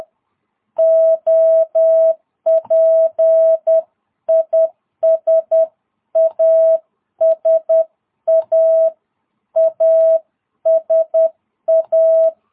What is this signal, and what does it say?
Morse code, heard as a keyed tone, about 10 wpm, 650 Hz. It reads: OP IS ASA ASA
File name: jh8cbh / 2022cw